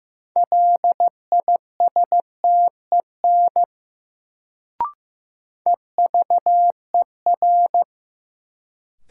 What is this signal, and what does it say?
Morse code, heard as a keyed tone, about 15 wpm, 700 Hz. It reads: LISTEN EVER
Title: Sentences from Top 400 Words - New Word Drill - 15wpm